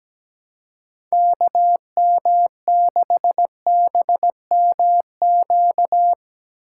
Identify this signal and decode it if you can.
Morse code, heard as a keyed tone, about 17 wpm, 700 Hz. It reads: KM6BMQ
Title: USA Call Signs - 6 Characters - 3x Character Spacing - 17wpm